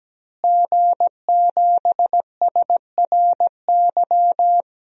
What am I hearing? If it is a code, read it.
Morse code, heard as a keyed tone, about 17 wpm, 700 Hz. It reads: G7SRY
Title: DXCC-100 Call Signs - 5 Characters - Repeated 2x - 17wpm